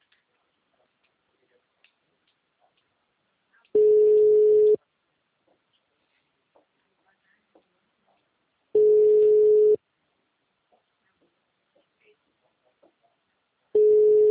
Ringback tone (Europe, 425 Hz).